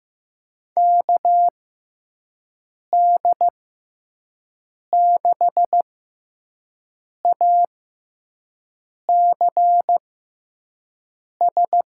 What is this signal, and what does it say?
Morse code, heard as a keyed tone, about 15 wpm, spaced as for 5 wpm, 700 Hz. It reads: KD6ACS
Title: USA Call Signs - 6x Character Spacing - 15wpm